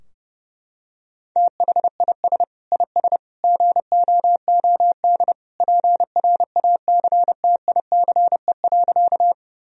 Morse code, heard as a keyed tone, about 30 wpm, 700 Hz. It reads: THIS IS GOOD PRACTICE.
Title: Sentences from Top 1000 Words - Review Part 4 - 30wpm